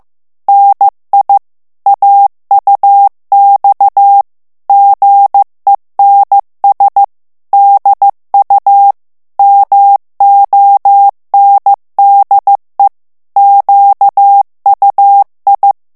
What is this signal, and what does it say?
Morse code, heard as a keyed tone, about 15 wpm, 800 Hz. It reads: NI AUX GENS DU MONDE QUI
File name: FRANCAIS TEXTE CLAIR 2 VITESSE 14 MOTS MINUTES